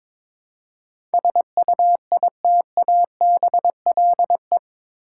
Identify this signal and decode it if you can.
Morse code, heard as a keyed tone, about 22 wpm, 700 Hz. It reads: SUITABLE
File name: practice / 22wpm / Eight Letter Words - 22wpm